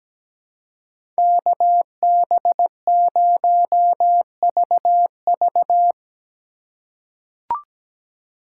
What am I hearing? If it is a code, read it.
Morse code, heard as a keyed tone, about 17 wpm, 700 Hz. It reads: KB0VV
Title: USA Call Signs - 5 Characters - 4x Character Spacing - 17wpm